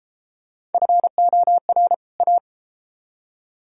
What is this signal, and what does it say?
Morse code, heard as a keyed tone, about 31 wpm, 700 Hz. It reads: FOR A